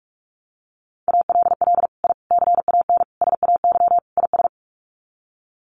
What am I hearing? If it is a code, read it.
Morse code, heard as a keyed tone, about 44 wpm, 690 Hz. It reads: ALL I CAN SAY IS
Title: Sets of 5 Words Encore2 45wpm